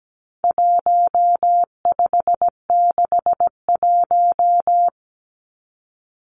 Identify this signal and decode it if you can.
Morse code, heard as a keyed tone, about 17 wpm, 700 Hz. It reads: 1561
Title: Numbers - 4-Digit 17wpm